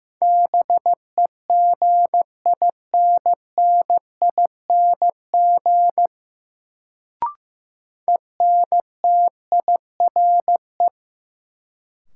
Morse code, heard as a keyed tone, about 15 wpm, 700 Hz. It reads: BEGINNING ENTIRE
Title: Sentences from Top 800 Words - New Word Drill - 15wpm